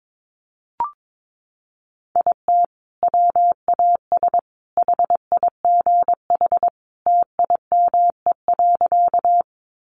Morse code, heard as a keyed tone, about 22 wpm, 700 Hz. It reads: IT WAS HIGH TIME.
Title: Sentences from Top 200 Words 22wpm